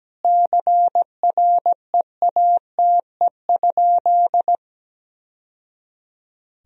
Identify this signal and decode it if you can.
Morse code, heard as a keyed tone, about 17 wpm, 700 Hz. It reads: CREATE?
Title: Sentences from Top 1000 Words - Review Part 3 - 17wpm